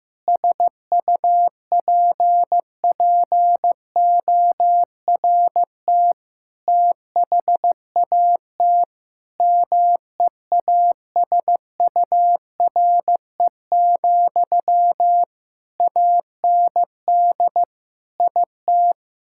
Morse code, heard as a keyed tone, about 15 wpm, 700 Hz. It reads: SUPPORT THAT MEASURE, AND IT
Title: Sentences from Top 1000 Words - Review Part 5 - 15wpm